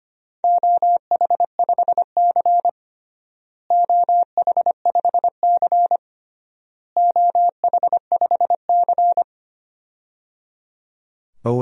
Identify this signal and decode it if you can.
Morse code, heard as a keyed tone, about 25 wpm, 700 Hz. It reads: OH5C OH5C OH5C
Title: DXCC-100 Call Signs - 4 Characters - Repeated 3x - 25wpm